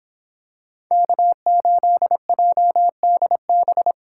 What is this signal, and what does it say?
Morse code, heard as a keyed tone, about 26 wpm, 700 Hz. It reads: K8JDB